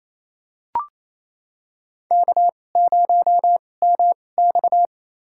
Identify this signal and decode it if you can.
Morse code, heard as a keyed tone, about 28 wpm, spaced as for 20 wpm, 700 Hz. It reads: K0MX
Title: USA Call Signs - 4 Characters - 2x Character Spacing - 28wpm